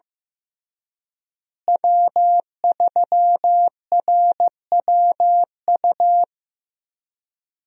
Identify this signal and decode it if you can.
Morse code, heard as a keyed tone, about 15 wpm, 700 Hz. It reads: W3RWU
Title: USA Call Signs - 5 Characters - Repeated 3x - 15wpm